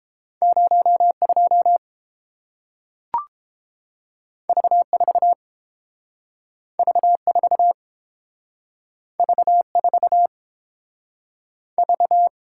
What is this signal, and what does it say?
Morse code, heard as a keyed tone, about 31 wpm, 700 Hz. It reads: 0 2 V 4 V 4 V 4 V